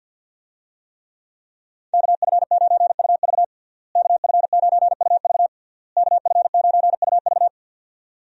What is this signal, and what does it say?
Morse code, heard as a keyed tone, about 46 wpm, 700 Hz. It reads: KF9UV KF9UV KF9UV
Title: USA Call Signs - 5 Characters - Repeated 3x - 50wpm